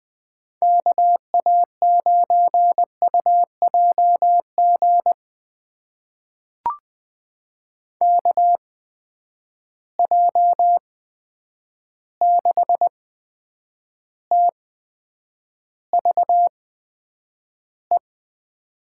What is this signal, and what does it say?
Morse code, heard as a keyed tone, about 20 wpm, spaced as for 9 wpm, 700 Hz. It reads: KA9UJG KJ6TVE
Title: USA Call Signs - 6 Characters - 8x Character Spacing - 20wpm